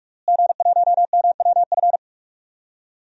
Morse code, heard as a keyed tone, about 46 wpm, 700 Hz. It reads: G1MWF
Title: DXCC-100 Call Signs - 5 Characters - Repeated 3x - 45wpm